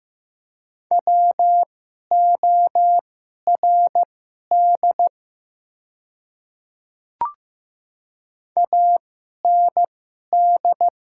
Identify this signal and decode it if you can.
Morse code, heard as a keyed tone, about 15 wpm, spaced as for 11 wpm, 700 Hz. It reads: WORD AND
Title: 2x Character Spacing - Top 300 Words - 15wpm